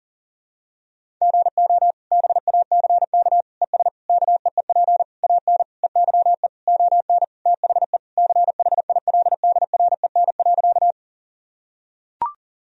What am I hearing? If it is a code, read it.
Morse code, heard as a keyed tone, about 40 wpm, 700 Hz. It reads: GO BACK ES KEEP AN EYE ON THE CHILDREN.
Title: Sentences from Top 300 Words and Abbreviations 40wpm